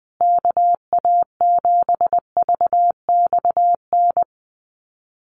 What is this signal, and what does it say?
Morse code, heard as a keyed tone, about 20 wpm, 700 Hz. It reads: KA7VXN
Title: USA Call Signs Repeated 3x 20wpm